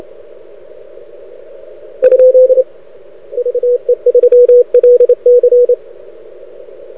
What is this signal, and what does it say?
Morse code, heard as a keyed tone, about 28 wpm, 500 Hz. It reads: ? VE3LC
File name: rec contest / arrl cw 2018 / test 32